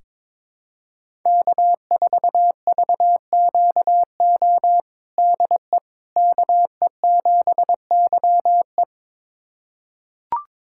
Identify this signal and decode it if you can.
Morse code, heard as a keyed tone, about 21 wpm, 700 Hz. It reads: K4VQO DE KE7YE
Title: USA Callsign DE Callsign - 5 Characters - 22wpm